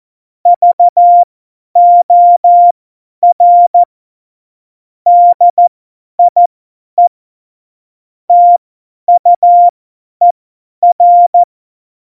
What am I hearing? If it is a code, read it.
Morse code, heard as a keyed tone, about 14 wpm, spaced as for 10 wpm, 700 Hz. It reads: VOR DIE TUER
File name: Morsekurs / Witze / Witz 22. Dezember 2020, 50 F70